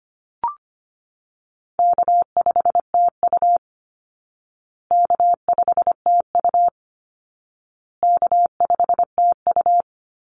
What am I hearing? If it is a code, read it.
Morse code, heard as a keyed tone, about 25 wpm, 700 Hz. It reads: K5TU K5TU K5TU
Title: Call Signs Repeated 2x - 25wpm